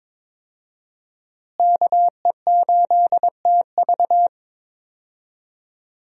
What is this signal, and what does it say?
Morse code, heard as a keyed tone, about 21 wpm, 700 Hz. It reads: KE8TV
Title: USA Call Signs - 5 Characters - Repeated 2x - 22wpm